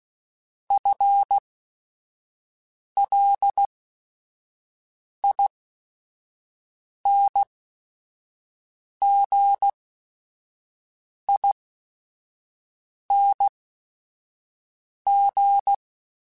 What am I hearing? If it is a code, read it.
Morse code, heard as a keyed tone, about 16 wpm, spaced as for 5 wpm, 800 Hz. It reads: FLINGING